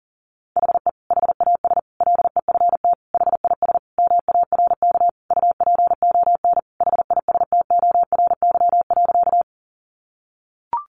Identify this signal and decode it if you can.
Morse code, heard as a keyed tone, about 40 wpm, 700 Hz. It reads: HE HAS LEFT HIS MARK UPON HISTORY.
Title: Sentences from Top 1100 Words 40wpm